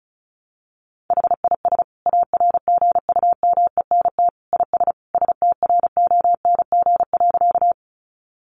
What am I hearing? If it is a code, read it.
Morse code, heard as a keyed tone, about 34 wpm, 700 Hz. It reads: HIS ARGUMENT IS STRONG.
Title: Sentences from Top 1400 Words 35wpm